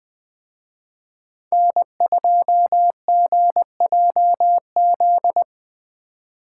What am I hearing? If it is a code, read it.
Morse code, heard as a keyed tone, about 20 wpm, 700 Hz. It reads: N2GJZ
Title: Speed Racing - USA 1x3 Call Signs 17wpm